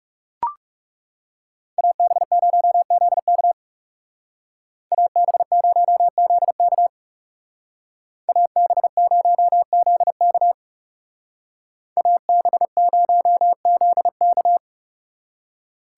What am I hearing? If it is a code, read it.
Morse code, heard as a keyed tone, about 39 wpm, 700 Hz. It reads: AB0ZK AB0ZK AB0ZK AB0ZK